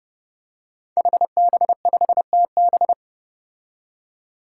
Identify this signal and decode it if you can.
Morse code, heard as a keyed tone, about 32 wpm, 700 Hz. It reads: HB5TB
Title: Five Challenging Letter-Number Combinations 30wpm